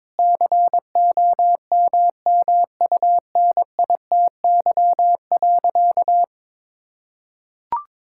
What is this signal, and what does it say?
Morse code, heard as a keyed tone, about 22 wpm, 700 Hz. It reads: COMMUNITY.
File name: practice / 22wpm / Sentences with 8-Word Phrases - 22wpm